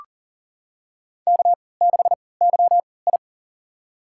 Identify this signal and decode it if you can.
Morse code, heard as a keyed tone, about 40 wpm, spaced as for 23 wpm, 700 Hz. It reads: K6YI
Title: USA Call Signs - 4 Characters - 3x Character Spacing - 40wpm